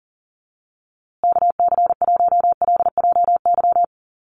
Morse code, heard as a keyed tone, about 40 wpm, 700 Hz. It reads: KC1LJY
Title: USA Call Signs Repeated 3x 40wpm